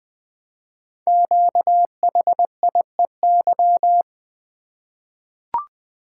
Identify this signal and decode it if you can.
Morse code, heard as a keyed tone, about 20 wpm, 700 Hz. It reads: QHIEY
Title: Code Groups - 5 Letters - 20wpm